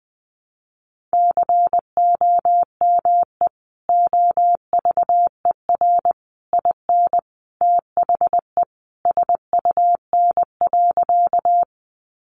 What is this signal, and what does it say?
Morse code, heard as a keyed tone, about 20 wpm, 700 Hz. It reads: COME OVER IN THE SUN.
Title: Sentences from Top 300 Words 20wpm